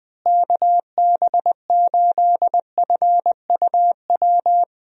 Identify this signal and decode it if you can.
Morse code, heard as a keyed tone, about 20 wpm, 700 Hz. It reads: KB8FUW
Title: USA Call Signs - 6 Characters - Repeated 3x - 20wpm